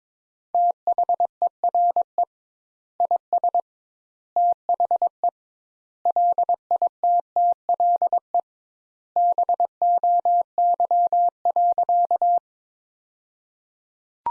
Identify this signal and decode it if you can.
Morse code, heard as a keyed tone, about 22 wpm, 700 Hz. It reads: THERE IS THE LITTLE BOY.
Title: Sentences Top 200 Words - 22wpm